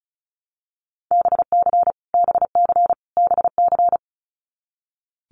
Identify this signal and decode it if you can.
Morse code, heard as a keyed tone, about 34 wpm, 700 Hz. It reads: BC BC BC
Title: Canadian Provinces and Territories - Repeated 3x - 35wpm